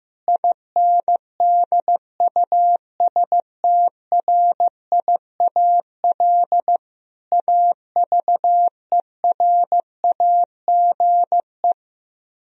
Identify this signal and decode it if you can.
Morse code, heard as a keyed tone, about 15 wpm, 700 Hz. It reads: INDUSTRIAL AVERAGE
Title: Speed-Racing - Sets of 5 Words 15wpm